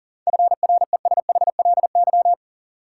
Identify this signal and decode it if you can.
Morse code, heard as a keyed tone, about 38 wpm, 700 Hz. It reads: FRESHLY
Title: Sets of 5 Words Encore - New Word Drill - 40wpm